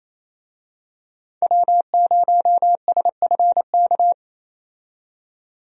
Morse code, heard as a keyed tone, about 28 wpm, 700 Hz. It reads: W0SFK
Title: USA Call Signs - 5 Characters - 28wpm